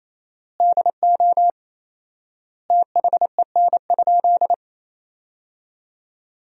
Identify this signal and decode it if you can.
Morse code, heard as a keyed tone, about 28 wpm, 700 Hz. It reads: DO THEN?
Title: 4x Word-Spacing - Sentences from Top 100 Words - 28wpm